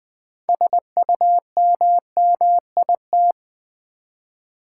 Morse code, heard as a keyed tone, about 20 wpm, 700 Hz. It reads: SUMMIT
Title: 6 Letter Words - First 3 Letters Given - 20wpm